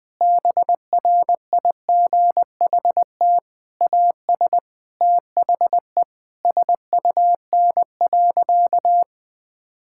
Morse code, heard as a keyed tone, about 21 wpm, 700 Hz. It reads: BRIGHT AS THE SUN.
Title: Difficult Sentences with Hints - 20wpm